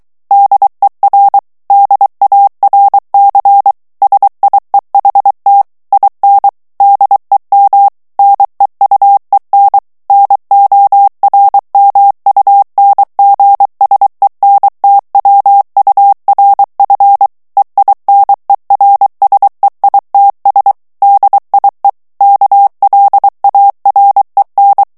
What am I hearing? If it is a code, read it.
Morse code, heard as a keyed tone, about 23 wpm, 800 Hz. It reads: DER DARC SIEHT IN DEM NEUEN NORMUNGSENTWURF EINERSEITS DIE KLAREN
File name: ALLEMAND texte clair 1 vitesse 22 mots minute